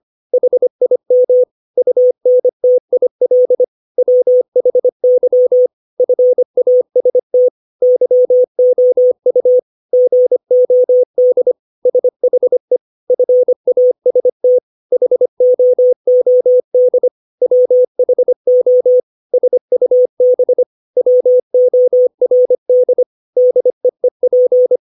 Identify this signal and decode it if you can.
Morse code, heard as a keyed tone, about 25 wpm, 500 Hz. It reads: HIM UNTIL WHY FAST YOU GOD SHE FAST HOOD WHO SUB WORD DEEP
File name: 632 words / 632 Words 25WPM